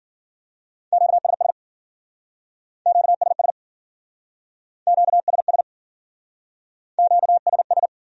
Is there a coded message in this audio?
Yes: Morse code from 0.9 to 7.9 s, heard as a keyed tone, about 51 wpm, 700 Hz. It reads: QSS QSS QSS QSS